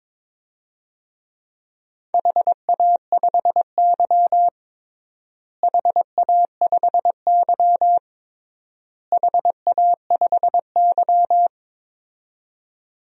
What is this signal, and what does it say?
Morse code, heard as a keyed tone, about 23 wpm, 700 Hz. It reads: HA5Y HA5Y HA5Y
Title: DXCC-100 Call Signs - 4 Characters - Repeated 3x - 22wpm